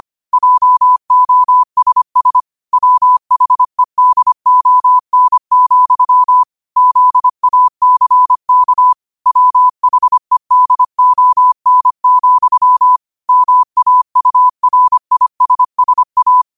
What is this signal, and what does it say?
Morse code, heard as a keyed tone, about 25 wpm, 1000 Hz. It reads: JOSS WHEDON, ZACK WHEDON, MAURISSA
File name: qotdfiles / 25wpm